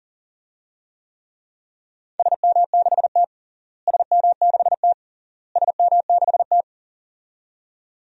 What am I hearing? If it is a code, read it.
Morse code, heard as a keyed tone, about 40 wpm, 700 Hz. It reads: SM6T SM6T SM6T
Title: DXCC-100 Call Signs - 4 Characters - Repeated 3x - 40wpm